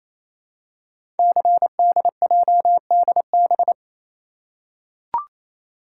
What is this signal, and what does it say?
Morse code, heard as a keyed tone, about 29 wpm, 700 Hz. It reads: CDJDB
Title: Code Groups - 5 Letters - 28wpm